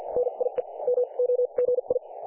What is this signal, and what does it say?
Morse code, heard as a keyed tone, about 50 wpm, spaced as for 40 wpm, 500 Hz. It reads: NIE MOGI
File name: dlazaawansowanych / nagraniazpasma / websdr recording 3564.7kHz